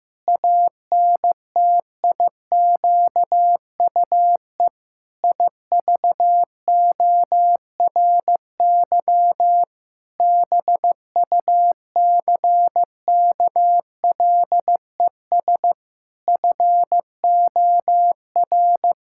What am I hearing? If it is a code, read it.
Morse code, heard as a keyed tone, about 15 wpm, 700 Hz. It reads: ANTIQUE IVORY BUCKLES FOR
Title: Pangram Challenge v2 - 15wpm